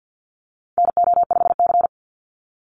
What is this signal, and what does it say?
Morse code, heard as a keyed tone, about 50 wpm, 700 Hz. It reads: NO5C
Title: USA Call Signs 50wpm